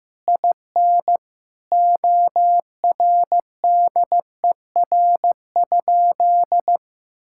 Morse code, heard as a keyed tone, about 15 wpm, 700 Hz. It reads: IN ORDER?
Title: Sentences from Top 1000 Words - Review Part 6 - 15wpm